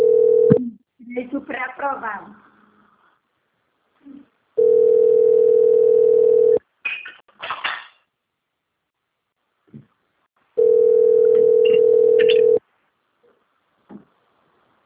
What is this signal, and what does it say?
Ringback tone (North America), 440 + 480 Hz, with speech.
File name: out- - - - - .23625